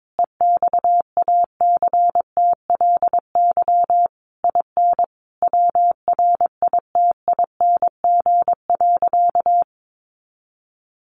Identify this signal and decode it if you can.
Morse code, heard as a keyed tone, about 22 wpm, 700 Hz. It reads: EXACTLY IN WRITING.
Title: Sentences from Top 900 Words 22wpm